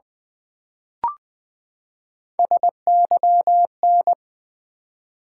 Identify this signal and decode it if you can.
Morse code, heard as a keyed tone, about 20 wpm, 700 Hz. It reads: SYN
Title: Build A Word - 3 Letters At A Time - 9-Letter Words - 20wpm